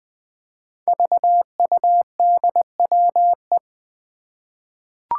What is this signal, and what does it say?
Morse code, heard as a keyed tone, about 20 wpm, 700 Hz. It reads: VUDWE
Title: Five Challenging Letter-Number Combinations 20wpm